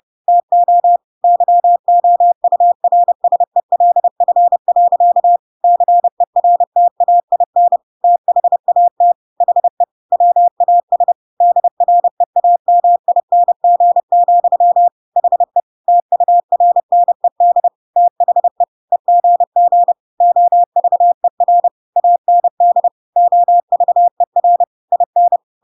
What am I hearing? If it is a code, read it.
Morse code, heard as a keyed tone, about 30 wpm, 700 Hz. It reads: TO YOURSELF. CERTAIN THAT HE WAS DREAMING, HE TURNED THE EGG OVER AND OVER IN
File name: Pinocchio / Pinocchio ch5 30 30 ik1hge